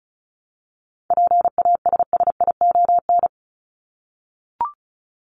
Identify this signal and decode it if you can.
Morse code, heard as a keyed tone, about 34 wpm, 700 Hz. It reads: PASSION
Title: Top 1001-1250 Words 35wpm